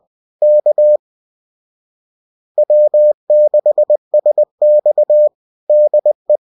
Morse code, heard as a keyed tone, about 20 wpm, 600 Hz. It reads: K W6SX DE